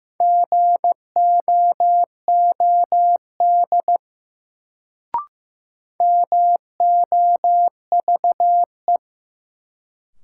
Morse code, heard as a keyed tone, about 15 wpm, 700 Hz. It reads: GOOD MOVE
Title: Sentences from Top 200 Words - New Word Drill - 15wpm